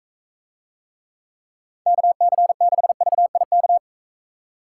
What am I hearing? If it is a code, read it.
Morse code, heard as a keyed tone, about 43 wpm, 700 Hz. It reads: KC6VIK